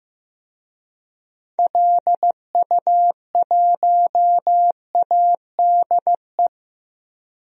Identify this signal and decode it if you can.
Morse code, heard as a keyed tone, about 16 wpm, 700 Hz. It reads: LU1ADE
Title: DXCC-100 Call Signs - 6 Characters - Repeated 2x - 15wpm